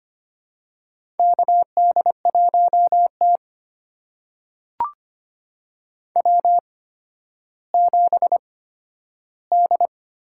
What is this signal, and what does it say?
Morse code, heard as a keyed tone, about 25 wpm, spaced as for 12 wpm, 700 Hz. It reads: KD1T W7D
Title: USA Call Signs - 4 Characters - 8x Character Spacing - 25wpm